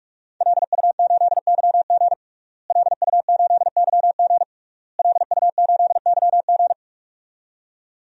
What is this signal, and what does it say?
Morse code, heard as a keyed tone, about 41 wpm, 700 Hz. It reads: LU8YG LU8YG LU8YG
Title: DXCC-100 Call Signs - 5 Characters - Repeated 3x - 45wpm